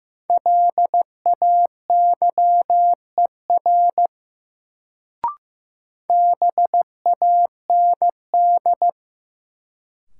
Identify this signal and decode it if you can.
Morse code, heard as a keyed tone, about 15 wpm, 700 Hz. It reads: LAYER BAND
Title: Sentences from Top 1600 Words - New Word Drill - 15wpm